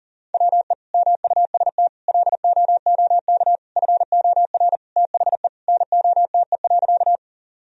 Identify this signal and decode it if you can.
Morse code, heard as a keyed tone, about 37 wpm, 700 Hz. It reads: WE MUST LOOK FOR THE NOTE.
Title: Sentences from Top 1000 Words - Review Part 2 - 40wpm